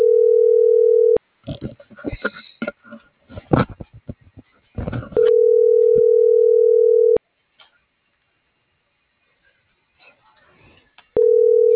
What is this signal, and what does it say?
Ringback tone (North America), 440 + 480 Hz.